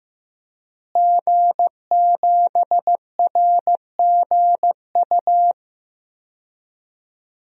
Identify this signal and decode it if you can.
Morse code, heard as a keyed tone, about 15 wpm, 700 Hz. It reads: G7RGU
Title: DXCC-100 Call Signs - 5 Characters - Repeated 3x - 15wpm